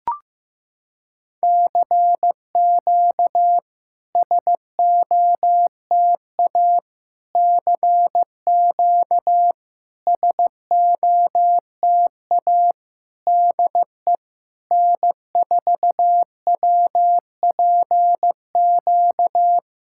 Morse code, heard as a keyed tone, about 16 wpm, 700 Hz. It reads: CQ SOTA CQ SOTA DE N4WPQ
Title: SOTA QSO Practice 15wpm